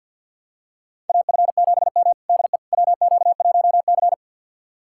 Morse code, heard as a keyed tone, about 50 wpm, 700 Hz. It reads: AF7M DE WQ1C
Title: USA Callsign DE Callsign - 4 Characters - 50wpm